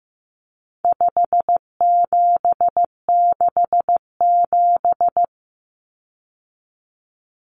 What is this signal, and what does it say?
Morse code, heard as a keyed tone, about 15 wpm, 700 Hz. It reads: 5767
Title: Common Cut Numbers - 4-Digit 15wpm